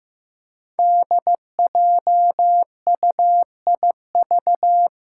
Morse code, heard as a keyed tone, about 15 wpm, 700 Hz. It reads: DJUIV
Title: Five Challenging Letter-Number Combinations 15wpm